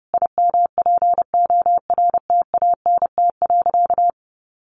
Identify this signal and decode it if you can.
Morse code, heard as a keyed tone, about 30 wpm, 700 Hz. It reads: IMPORTANT.
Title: Sentences from Top 1200 Words 30wpm